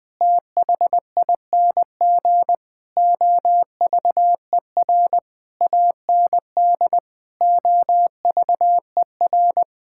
Morse code, heard as a keyed tone, about 21 wpm, 700 Hz. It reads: THING OVER AND OVER
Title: Sets of 8 Words - 20wpm